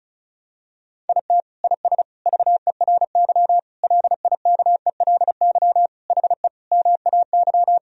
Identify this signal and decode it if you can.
Morse code, heard as a keyed tone, about 34 wpm, 700 Hz. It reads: IT IS VERY LIKELY HE MAY
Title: Sentences from Top 1000 Words - Review Part 4 - 35wpm